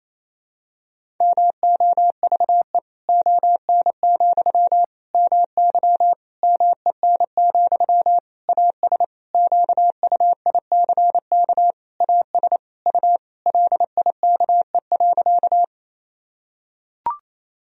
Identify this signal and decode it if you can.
Morse code, heard as a keyed tone, about 28 wpm, 700 Hz. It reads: MOVE ON, MY MEN, AS QUICK AS U LIKE.
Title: Sentences from Top 500 Words and Abbreviations 28wpm